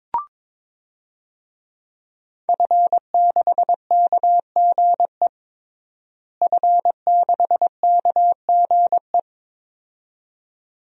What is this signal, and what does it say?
Morse code, heard as a keyed tone, about 22 wpm, 700 Hz. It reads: F6KGE F6KGE
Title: DXCC-100 Call Signs - 5 Characters - Repeated 2x - 22wpm